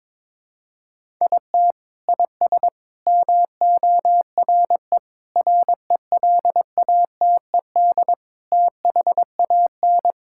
Morse code, heard as a keyed tone, about 22 wpm, 700 Hz. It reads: IT IS MORE RELATED THAN
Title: Sentences from Top 1000 Words - Review Part 5 - 22wpm